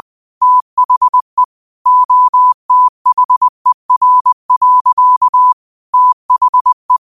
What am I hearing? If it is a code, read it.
Morse code, heard as a keyed tone, about 20 wpm, 1000 Hz. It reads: THE OTHER. THE
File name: qotdfiles / 20wpm